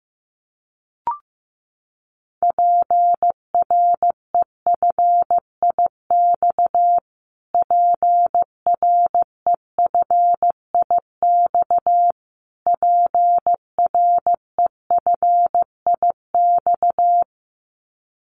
Morse code, heard as a keyed tone, about 15 wpm, 700 Hz. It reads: PREFIX PREFIX PREFIX
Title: Callsign Prefixes for Top 25 DXCC - Repeated 3x - 15wpm